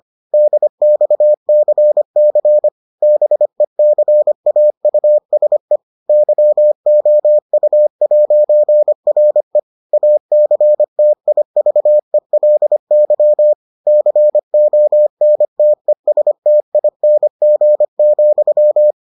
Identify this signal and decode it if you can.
Morse code, heard as a keyed tone, about 25 wpm, 600 Hz. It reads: DXCC BECAUSE YOU'RE ACTIVELY CONTESTING,